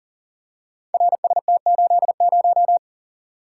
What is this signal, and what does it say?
Morse code, heard as a keyed tone, about 43 wpm, 700 Hz. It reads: RST80
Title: Code Groups - 5 Letter:Numbers - 40wpm